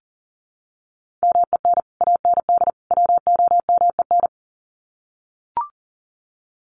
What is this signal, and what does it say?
Morse code, heard as a keyed tone, about 40 wpm, 700 Hz. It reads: MEN AND WOMEN
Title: Sets of 3 Words 40wpm